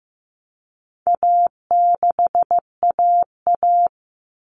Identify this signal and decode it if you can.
Morse code, heard as a keyed tone, about 15 wpm, 700 Hz. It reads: A6AA
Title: All Cut Numbers - 4-Digit 15wpm